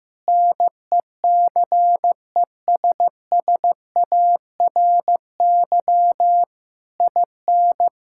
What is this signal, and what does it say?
Morse code, heard as a keyed tone, about 15 wpm, 700 Hz. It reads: NECESSARY IN
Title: Sentences from Top 1000 Words - Review Part 6 - 15wpm